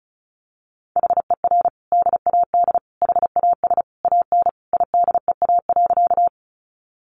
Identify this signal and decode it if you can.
Morse code, heard as a keyed tone, about 34 wpm, 700 Hz. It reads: HER DAD HAS AN IDEA.
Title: Easy Phrases 35wpm